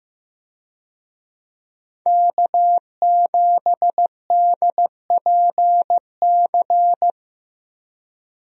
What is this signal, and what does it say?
Morse code, heard as a keyed tone, about 15 wpm, 700 Hz. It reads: K7DPC
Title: USA Call Signs - 5 Characters - Repeated 2x - 15wpm